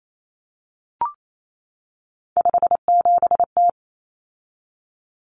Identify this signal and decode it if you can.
Morse code, heard as a keyed tone, about 28 wpm, 700 Hz. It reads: 57T
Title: Common Cut Numbers - 3-Digit 28wpm